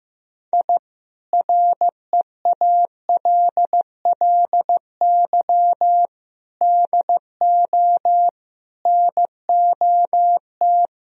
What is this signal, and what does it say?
Morse code, heard as a keyed tone, about 15 wpm, 700 Hz. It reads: I REALLY DO NOT
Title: Speed-Racing - Sets of 4 Words 15wpm